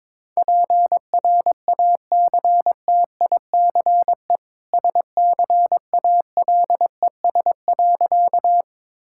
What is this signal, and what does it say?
Morse code, heard as a keyed tone, about 22 wpm, 690 Hz. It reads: PRACTICE SCALES.
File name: practice / 22wpm / Easy Sentences with Hints - 22wpm